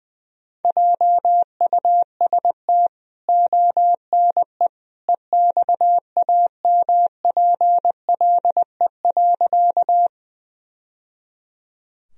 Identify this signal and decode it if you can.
Morse code, heard as a keyed tone, about 21 wpm, 700 Hz. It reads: JUST ONE EXAMPLE.
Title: Sentences from Top 1000 Words - Review Part 1 - 20wpm